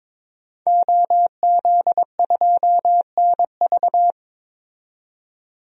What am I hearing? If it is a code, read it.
Morse code, heard as a keyed tone, about 22 wpm, 700 Hz. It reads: OZ2NV
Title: DXCC-100 Call Signs - 5 Characters - 22wpm